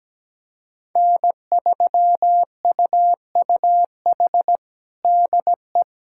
Morse code, heard as a keyed tone, about 17 wpm, 700 Hz. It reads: N3UUH DE
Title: USA Callsign DE Callsign - 5 Characters - 17wpm